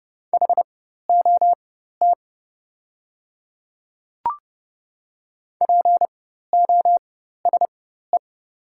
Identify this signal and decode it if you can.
Morse code, heard as a keyed tone, about 30 wpm, spaced as for 14 wpm, 700 Hz. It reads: HOT POSE
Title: 4x Character Spacing - Top 500 Words - 30wpm